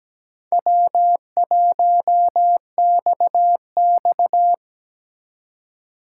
Morse code, heard as a keyed tone, about 20 wpm, 700 Hz. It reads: W1XX
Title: Speed Racing - USA Call Signs 17wpm